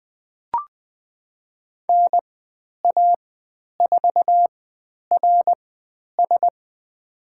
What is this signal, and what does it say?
Morse code, heard as a keyed tone, about 20 wpm, spaced as for 10 wpm, 700 Hz. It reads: NA4RS
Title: 102 Lesson 17g - Letter u - Call Signs 20wpm